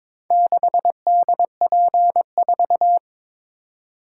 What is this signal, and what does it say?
Morse code, heard as a keyed tone, about 22 wpm, 700 Hz. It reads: 6DP4
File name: practice / 22wpm / Four Letter-Number Combinations - 22wpm